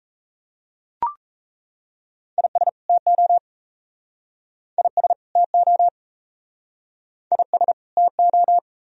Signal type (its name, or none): Morse code